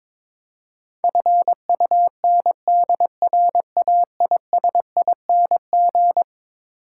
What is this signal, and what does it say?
Morse code, heard as a keyed tone, about 22 wpm, 700 Hz. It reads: FUNDRAISING
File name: practice / 22wpm / Eleven Letter Words - 22wpm